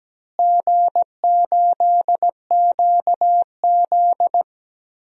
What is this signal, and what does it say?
Morse code, heard as a keyed tone, about 17 wpm, 700 Hz. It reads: G8QZ
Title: DXCC-100 Call Signs - 4 Characters - Repeated 2x - 17wpm